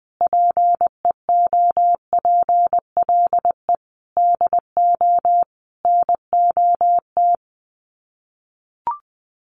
Morse code, heard as a keyed tone, about 20 wpm, 700 Hz. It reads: PEOPLE DO NOT
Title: Sets of 5 Words Encore 20wpm